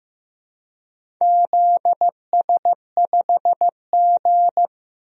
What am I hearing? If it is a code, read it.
Morse code, heard as a keyed tone, about 15 wpm, 700 Hz. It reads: ZS5G
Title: DXCC-100 Call Signs - 4 Characters - 15wpm